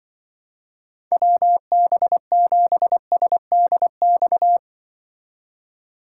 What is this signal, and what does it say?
Morse code, heard as a keyed tone, about 25 wpm, 700 Hz. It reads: WB7SDX
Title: Speed Racing - USA 2x3 Call Signs 20wpm